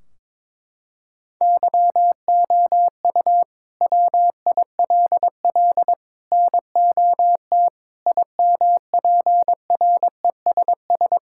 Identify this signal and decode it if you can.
Morse code, heard as a keyed tone, about 21 wpm, 700 Hz. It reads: YOU WILL NOT IMPRESS
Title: Sentences from Top 1600 Words 22wpm